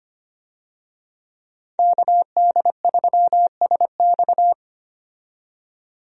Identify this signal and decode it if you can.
Morse code, heard as a keyed tone, about 25 wpm, 700 Hz. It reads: KD3SX
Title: USA Call Signs - 5 Characters - 25wpm